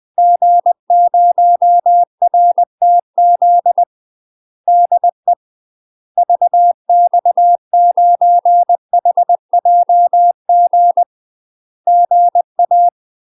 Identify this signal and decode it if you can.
Morse code, heard as a keyed tone, about 20 wpm, 700 Hz. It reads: G0RTZ DE VX9HJG GA
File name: UK MASTER QSOs TRAINING / 20WPM / UK MASTER QSOs 0067